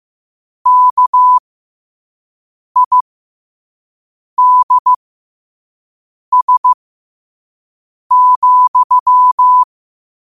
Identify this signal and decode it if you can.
Morse code, heard as a keyed tone, about 15 wpm, spaced as for 5 wpm, 1000 Hz. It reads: KIDS,
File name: qotdfiles / 5wpm